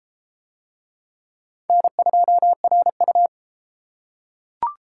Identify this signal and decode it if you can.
Morse code, heard as a keyed tone, about 30 wpm, 700 Hz. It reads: N2RU